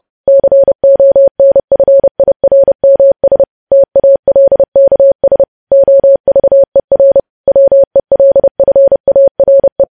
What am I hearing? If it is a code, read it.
Morse code, heard as a keyed tone, about 30 wpm, 550 Hz. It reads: CONFIRMS TALKS OVER WELFARE
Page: News Headlines in Morse Code at 30 WPM